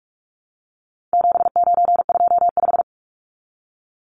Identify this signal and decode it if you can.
Morse code, heard as a keyed tone, about 44 wpm, 700 Hz. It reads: 7925
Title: Numbers - 4-Digit 45wpm